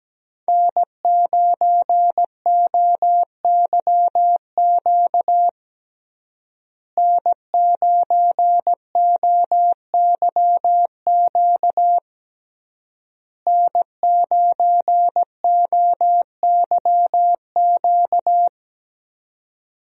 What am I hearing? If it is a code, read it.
Morse code, heard as a keyed tone, about 17 wpm, 700 Hz. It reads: N9OYQ N9OYQ N9OYQ
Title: USA Call Signs - 5 Characters - Repeated 3x - 17wpm